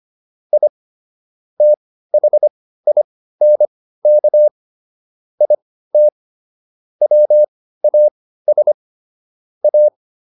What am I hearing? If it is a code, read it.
Morse code, heard as a keyed tone, about 25 wpm, spaced as for 15 wpm, 600 Hz. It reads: I THINK IT WAS A